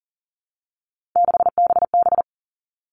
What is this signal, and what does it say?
Morse code, heard as a keyed tone, about 40 wpm, 700 Hz. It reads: 6BB
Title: All Cut Numbers - 3-Digit 40wpm